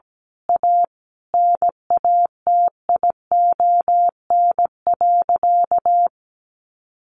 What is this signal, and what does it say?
Morse code, heard as a keyed tone, about 17 wpm, 700 Hz. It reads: A NATION.
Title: Sentences from Top 1400 Words 17wpm